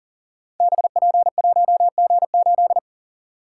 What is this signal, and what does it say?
Morse code, heard as a keyed tone, about 42 wpm, 690 Hz. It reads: BP1G8
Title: Code Groups - 5 Letter:Numbers - 40wpm